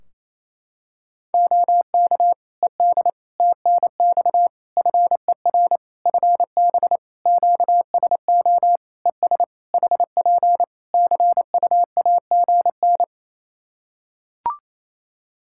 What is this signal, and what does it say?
Morse code, heard as a keyed tone, about 28 wpm, 700 Hz. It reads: OK ED TNX FER FB QSO ES HP CUAGN
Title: K1USN SST QSO Practice 28wpm